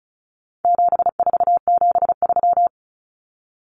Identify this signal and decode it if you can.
Morse code, heard as a keyed tone, about 34 wpm, 700 Hz. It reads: 7473
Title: Common Cut Numbers - 4-Digit 35wpm